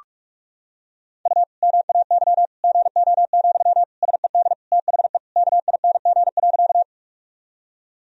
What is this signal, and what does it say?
Morse code, heard as a keyed tone, about 46 wpm, 700 Hz. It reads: U MAY GO, SED THE KING.
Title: Sentences from Top 400 Words and Abbreviations 45wpm